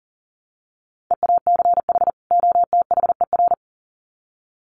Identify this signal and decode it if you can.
Morse code, heard as a keyed tone, about 40 wpm, 700 Hz. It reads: EACH OTHER
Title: Sets of 2 Words 40wpm